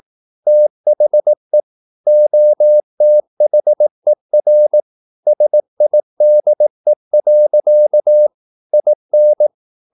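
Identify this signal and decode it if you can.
Morse code, heard as a keyed tone, about 18 wpm, 600 Hz. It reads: THE OTHER SIDE. IN
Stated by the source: Morse practice at 18 wpm, English text, from CW med Gnister.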